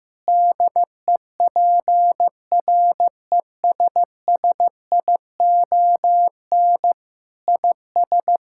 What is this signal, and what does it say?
Morse code, heard as a keyed tone, about 15 wpm, 700 Hz. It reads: DEPRESSION IS
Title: Sentences from Top 1600 Words 15wpm